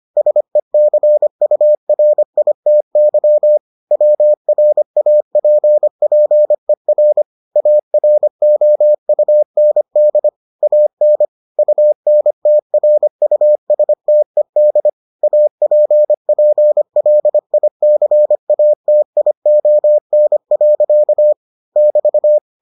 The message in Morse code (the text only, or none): SECURITY WRAPPER AROUND AN UNTRUSTED APPLICATION. <BT>